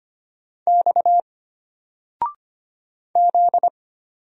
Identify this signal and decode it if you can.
Morse code, heard as a keyed tone, about 25 wpm, 700 Hz. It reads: X Z
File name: Single-Letter-Number-25wpm